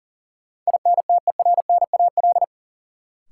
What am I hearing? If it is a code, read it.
Morse code, heard as a keyed tone, about 40 wpm, 700 Hz. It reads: INTERNAL
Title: Sentences from Top 1400 Words - New Word Drill - 40wpm